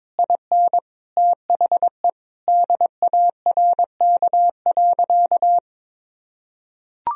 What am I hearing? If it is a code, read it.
Morse code, heard as a keyed tone, about 21 wpm, 700 Hz. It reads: IN THE DARK.
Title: Sentences from Top 1000 Words - Review Part 2 - 22wpm